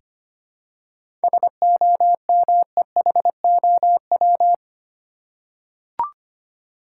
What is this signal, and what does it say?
Morse code, heard as a keyed tone, about 25 wpm, 700 Hz. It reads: SOMEHOW
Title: Seven Letter Words - 25wpm